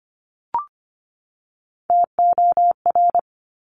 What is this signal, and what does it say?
Morse code, heard as a keyed tone, about 25 wpm, 700 Hz. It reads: TOR
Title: Top Three Letter Combinations in Words 25wpm